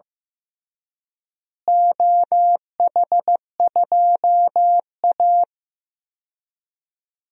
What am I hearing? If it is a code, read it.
Morse code, heard as a keyed tone, about 15 wpm, 700 Hz. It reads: OH2A
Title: DXCC-100 Call Signs - 4 Characters - Repeated 3x - 15wpm